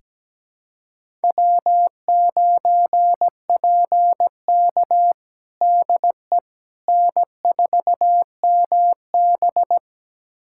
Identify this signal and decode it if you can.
Morse code, heard as a keyed tone, about 17 wpm, 700 Hz. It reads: W9PK DE N4MB
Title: USA Callsign DE Callsign - 4 Characters - 17wpm